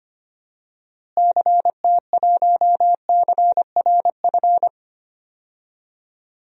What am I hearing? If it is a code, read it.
Morse code, heard as a keyed tone, about 25 wpm, 700 Hz. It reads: CT1CRF
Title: DXCC-100 Call Signs - 6 Characters - Repeated 2x - 25wpm